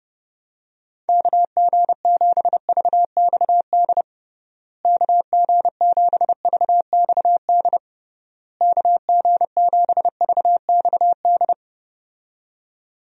Morse code, heard as a keyed tone, about 32 wpm, 700 Hz. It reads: KG7VXD KG7VXD KG7VXD
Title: USA Call Signs - 6 Characters - Repeated 3x - 30wpm